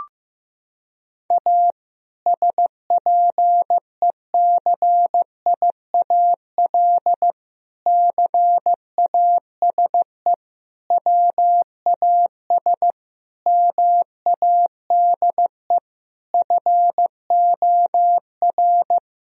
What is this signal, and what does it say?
Morse code, heard as a keyed tone, about 15 wpm, 700 Hz. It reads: A SPECIAL CASE WAS MADE FOR
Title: Sentences from Top 1000 Words - Review Part 3 - 15wpm